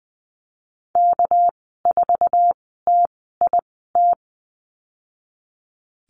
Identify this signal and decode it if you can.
Morse code, heard as a keyed tone, about 20 wpm, spaced as for 15 wpm, 700 Hz. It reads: K4TIT
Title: 207 Lesson 33g - Letter j - Call Signs 20wpm